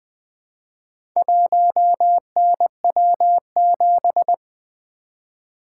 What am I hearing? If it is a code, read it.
Morse code, heard as a keyed tone, about 20 wpm, 700 Hz. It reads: 1NW7
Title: Four Letter-Number Combinations - 20wpm